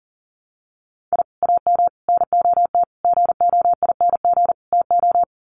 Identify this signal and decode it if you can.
Morse code, heard as a keyed tone, about 40 wpm, 700 Hz. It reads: I AM NOT GOING TO
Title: Sets of 5 Words Encore 40wpm